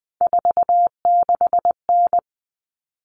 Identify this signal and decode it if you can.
Morse code, heard as a keyed tone, about 20 wpm, 700 Hz. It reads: 46N
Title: All Cut Numbers - 3-Digit 20wpm